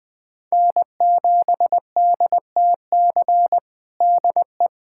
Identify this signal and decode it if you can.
Morse code, heard as a keyed tone, about 20 wpm, 700 Hz. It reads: N7DTC DE
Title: USA Callsign DE Callsign - 5 Characters - 20wpm